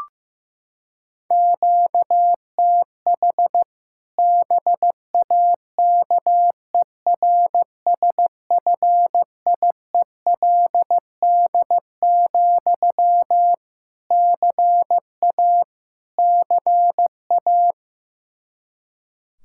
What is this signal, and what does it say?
Morse code, heard as a keyed tone, about 15 wpm, 700 Hz. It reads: QTH BAKERSFIELD, CA CA
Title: Practice QSO - Exchange 1 - Beginning - 15wpm